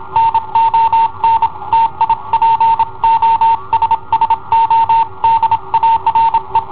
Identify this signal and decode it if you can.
Morse code, heard as a keyed tone, about 28 wpm, 910 Hz. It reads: NON TI POSSO DARE
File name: i7ohp / cassetta i7ohp B